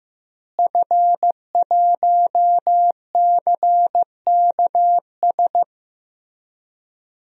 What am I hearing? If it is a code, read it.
Morse code, heard as a keyed tone, about 15 wpm, 700 Hz. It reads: F1CKS
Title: DXCC-100 Call Signs - 5 Characters - Repeated 3x - 15wpm